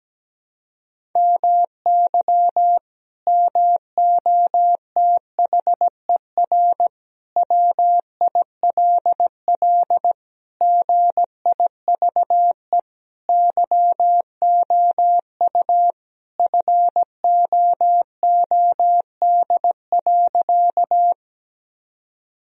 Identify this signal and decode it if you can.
Morse code, heard as a keyed tone, about 17 wpm, 700 Hz. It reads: MY MOTHER WILL GIVE YOU FOOD.
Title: Sentences from Top 1000 Words - Review Part 1 - 17wpm